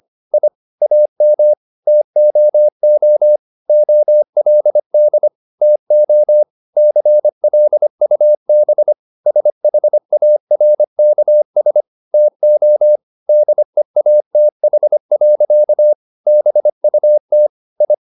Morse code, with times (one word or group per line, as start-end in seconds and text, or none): 0.3-0.5 I
0.8-1.5 AM
1.9-3.4 TOO
3.7-5.3 OLD
5.6-6.4 TO
6.8-8.9 CLUB
9.3-11.8 SHARKS
12.1-13.0 TO
13.3-15.9 DEATH.
16.3-17.5 BUT
17.8-17.9 I